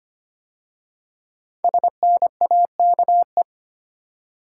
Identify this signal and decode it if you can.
Morse code, heard as a keyed tone, about 25 wpm, 700 Hz. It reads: SNAKE
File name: 5 Letter Words - First 3 Letters Given - 25wpm